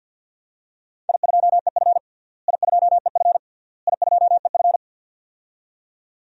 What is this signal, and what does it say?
Morse code, heard as a keyed tone, about 52 wpm, 700 Hz. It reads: I2EF I2EF I2EF
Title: DXCC-100 Call Signs - 4 Characters - Repeated 3x - 50wpm